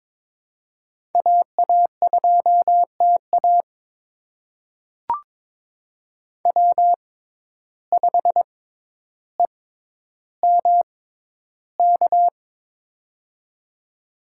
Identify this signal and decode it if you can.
Morse code, heard as a keyed tone, about 22 wpm, spaced as for 11 wpm, 700 Hz. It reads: AA2TA W5EMK
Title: USA Call Signs - 5 Characters - 6x Character Spacing - 22wpm